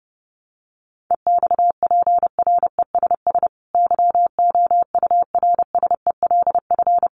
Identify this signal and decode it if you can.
Morse code, heard as a keyed tone, about 30 wpm, 700 Hz. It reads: EXPRESS YOURSELF
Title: Sentences from Top 1100 Words 30wpm